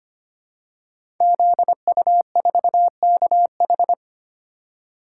Morse code, heard as a keyed tone, about 25 wpm, 700 Hz. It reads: ZU4KH
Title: Code Groups - 5 Letter:Numbers - 25wpm